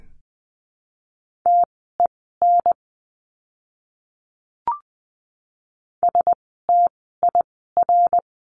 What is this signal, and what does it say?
Morse code, heard as a keyed tone, about 20 wpm, spaced as for 14 wpm, 700 Hz. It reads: TEN STIR
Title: 054 Lesson 10f - Letter r - Words 20wpm